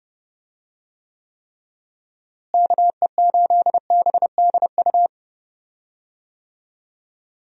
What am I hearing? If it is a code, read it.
Morse code, heard as a keyed tone, about 31 wpm, 700 Hz. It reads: KE8BDU
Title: USA Call Signs - 6 Characters - Repeated 3x - 30wpm